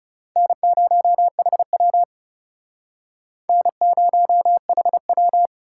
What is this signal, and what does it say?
Morse code, heard as a keyed tone, about 38 wpm, 700 Hz. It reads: N0HW N0HW